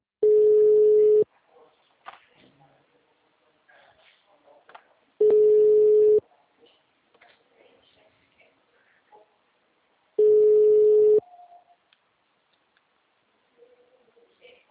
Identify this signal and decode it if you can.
Ringback tone (Europe, 425 Hz), with speech.
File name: out- -2001- - - .154